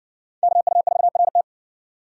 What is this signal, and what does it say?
Morse code, heard as a keyed tone, about 57 wpm, 700 Hz. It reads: KU4RT